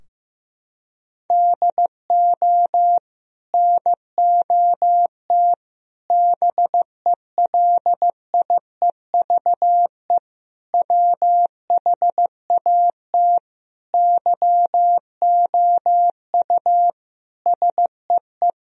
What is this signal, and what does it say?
Morse code, heard as a keyed tone, about 15 wpm, 700 Hz. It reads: DO NOT BELIEVE WHAT YOU SEE
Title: Sentences from Top 1000 Words - Review Part 4 - 15wpm